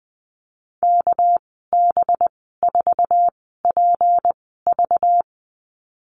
Morse code, heard as a keyed tone, about 20 wpm, spaced as for 14 wpm, 700 Hz. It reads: KB4PV
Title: 207 Lesson 33g - Letter j - Call Signs 20wpm